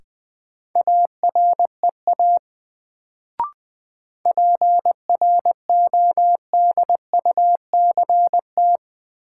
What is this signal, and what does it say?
Morse code, heard as a keyed tone, about 20 wpm, 700 Hz. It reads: AREA PRODUCT
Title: Sentences from Top 1000 Words - Review Part 2 - New Word Drill - 20wpm